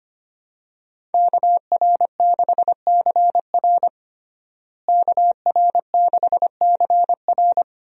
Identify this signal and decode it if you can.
Morse code, heard as a keyed tone, about 25 wpm, 700 Hz. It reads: KR6CR KR6CR
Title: USA Call Signs - 5 Characters - Repeated 3x - 25wpm